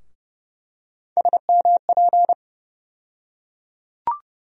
Morse code, heard as a keyed tone, about 31 wpm, 700 Hz. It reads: SMP
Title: USA Call Sign Postfixes - 3 Letters 30wpm